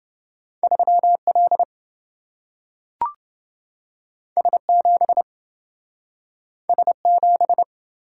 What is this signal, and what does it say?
Morse code, heard as a keyed tone, about 27 wpm, 700 Hz. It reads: 3L S7 S7